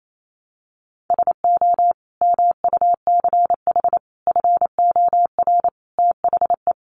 Morse code, heard as a keyed tone, about 28 wpm, 700 Hz. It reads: SO MUCH FOR THE
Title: Sentences from Top 1000 Words 28wpm